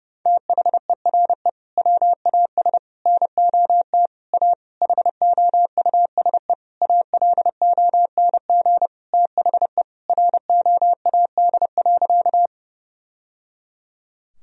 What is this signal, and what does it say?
Morse code, heard as a keyed tone, about 30 wpm, 700 Hz. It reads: THERE WAS NOT A HOUSE ALONG THE ROAD.
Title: Sentences from Top 1000 Words - Review Part 2 - 30wpm